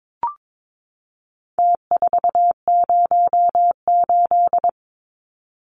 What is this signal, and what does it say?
Morse code, heard as a keyed tone, about 22 wpm, 700 Hz. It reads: T408
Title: Numbers - Contest Style - 4-Digit 22wpm